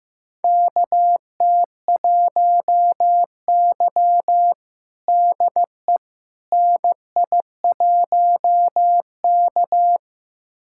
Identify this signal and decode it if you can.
Morse code, heard as a keyed tone, about 15 wpm, 700 Hz. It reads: KT1Y DE NI1K
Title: USA Callsign DE Callsign - 4 Characters - 15wpm